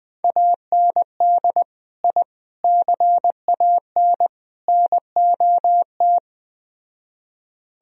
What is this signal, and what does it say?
Morse code, heard as a keyed tone, about 20 wpm, 700 Hz. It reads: AND I CAN NOT
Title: Speed-Racing - Sets of 4 Words 17wpm